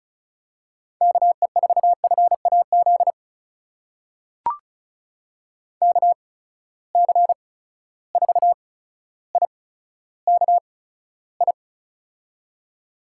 Morse code, heard as a keyed tone, about 36 wpm, spaced as for 15 wpm, 700 Hz. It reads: KE4FAZ KC4IKI
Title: USA Call Signs - 8x Character Spacing - 35wpm